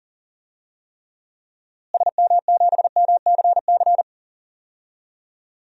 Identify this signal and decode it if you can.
Morse code, heard as a keyed tone, about 40 wpm, 700 Hz. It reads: SM7MCC
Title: DXCC-100 Call Signs - 6 Characters - Repeated 2x - 40wpm